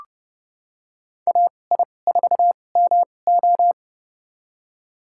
Morse code, heard as a keyed tone, about 30 wpm, spaced as for 22 wpm, 700 Hz. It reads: AI4MO
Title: USA Call Signs - 5 Characters - 2x Character Spacing - 30wpm